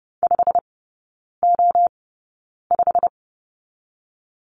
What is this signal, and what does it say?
Morse code, heard as a keyed tone, about 30 wpm, spaced as for 9 wpm, 700 Hz. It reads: 5O5
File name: course / 30:10wpm / 087 Lesson 15e - Number 5 - Three Characters 30wpm